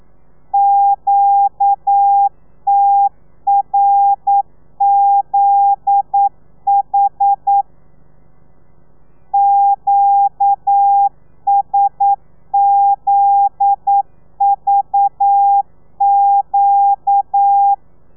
Morse code, heard as a keyed tone, about 9 wpm, 800 Hz. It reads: QTRZH QSZVQ
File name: telegrafia / i2tfj / Lez9 40